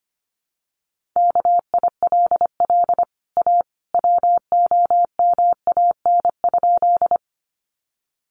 Morse code, heard as a keyed tone, about 25 wpm, 700 Hz. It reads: KILL A WOMAN?
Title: Sentences from Top 1400 Words 25wpm